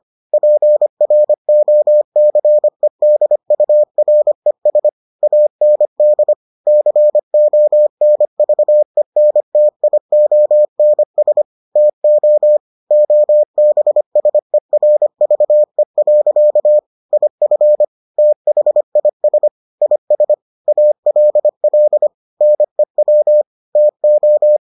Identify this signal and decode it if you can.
Morse code, heard as a keyed tone, about 25 wpm, 600 Hz. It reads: PROCEDURES AND CONVENTIONS TO OBSERVE. IF THIS IS ALL NEW TO